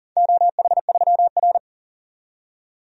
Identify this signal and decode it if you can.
Morse code, heard as a keyed tone, about 40 wpm, 700 Hz. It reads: OH3R
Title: DXCC-100 Call Signs - 4 Characters - 40wpm